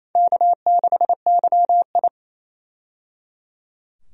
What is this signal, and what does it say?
Morse code, heard as a keyed tone, about 28 wpm, 700 Hz. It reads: K6YI
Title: USA Call Signs - 4 Characters - 28wpm